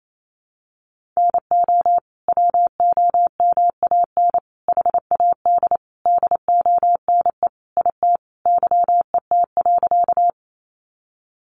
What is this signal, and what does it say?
Morse code, heard as a keyed tone, about 28 wpm, 700 Hz. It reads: NO WOMAN HAD DONE IT YET.
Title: Sentences from Top 1100 Words 28wpm